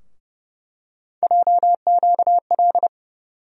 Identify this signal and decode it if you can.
Morse code, heard as a keyed tone, about 32 wpm, 700 Hz. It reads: JQL
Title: USA Call Sign Postfixes - 3 Letters 30wpm